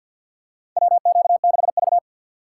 Morse code, heard as a keyed tone, about 51 wpm, 700 Hz. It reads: WQ6V
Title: USA Call Signs - 4 Characters - 2x Character Spacing - 50wpm